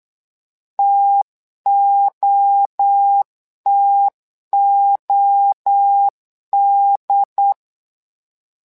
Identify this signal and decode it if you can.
Morse code, heard as a keyed tone, about 8 wpm, 800 Hz. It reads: TOTOD
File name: LEZIONI / 3 MWRO / 3 MWRO 8wpm 2606